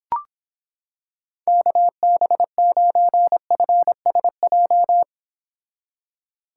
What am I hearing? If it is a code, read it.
Morse code, heard as a keyed tone, about 26 wpm, 700 Hz. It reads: KB9FSJ